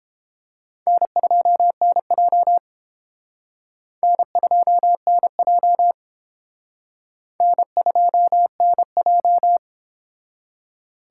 Morse code, heard as a keyed tone, about 30 wpm, 700 Hz. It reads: N2NJ N2NJ N2NJ